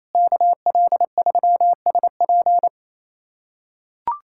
Morse code, heard as a keyed tone, about 28 wpm, 700 Hz. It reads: KL3SP
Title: USA Callsign DE Callsign - 5 Characters - 28wpm